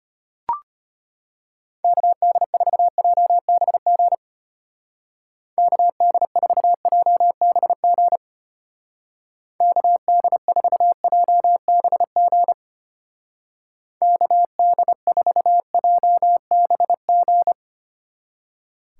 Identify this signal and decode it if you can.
Morse code, heard as a keyed tone, about 32 wpm, 700 Hz. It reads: KD4JBG KD4JBG KD4JBG KD4JBG